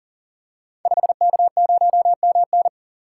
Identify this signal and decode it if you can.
Morse code, heard as a keyed tone, about 38 wpm, 700 Hz. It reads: 5K0MN
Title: Code Groups - 5 Letter:Numbers - 40wpm